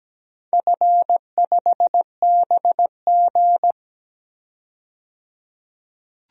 Morse code, heard as a keyed tone, about 17 wpm, 700 Hz. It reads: F5BG
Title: DXCC-100 Call Signs - 4 Characters - 17wpm